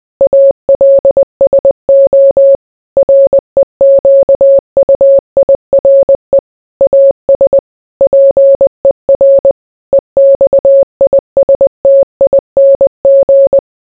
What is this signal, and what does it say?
Morse code, heard as a keyed tone, about 20 wpm, 550 Hz. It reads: ALSO REQUIRE AS PER EXISTING